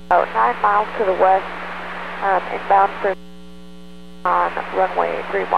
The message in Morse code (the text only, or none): none